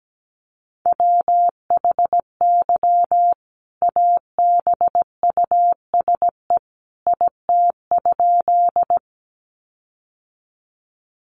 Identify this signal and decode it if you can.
Morse code, heard as a keyed tone, about 17 wpm, 700 Hz. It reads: WHY ABUSE IT?
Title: Sentences from Top 1300 Words 17wpm